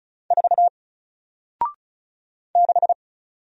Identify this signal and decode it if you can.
Morse code, heard as a keyed tone, about 36 wpm, 700 Hz. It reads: 4 6
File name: Single-Letter-Number-35wpm